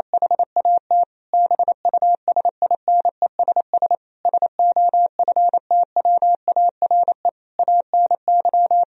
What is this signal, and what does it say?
Morse code, heard as a keyed tone, about 27 wpm, 700 Hz. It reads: HAT BUSINESS SOFTWARE ANY
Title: Sentences from Top 1000 Words - Review Part 3 - 28wpm